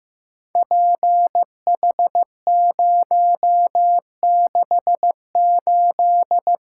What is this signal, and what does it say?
Morse code, heard as a keyed tone, about 15 wpm, 700 Hz. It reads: PH068
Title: Code Groups - 5 Letter:Numbers - 15wpm